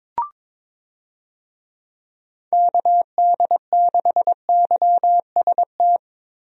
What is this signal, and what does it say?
Morse code, heard as a keyed tone, about 22 wpm, 700 Hz. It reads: KD6YST
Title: USA Call Signs - 6 Characters - Repeated 3x - 22wpm